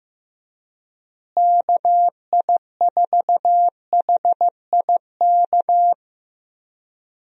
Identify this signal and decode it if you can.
Morse code, heard as a keyed tone, about 15 wpm, 700 Hz. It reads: KI4HIK
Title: USA Call Signs - 6 Characters - Repeated 3x - 15wpm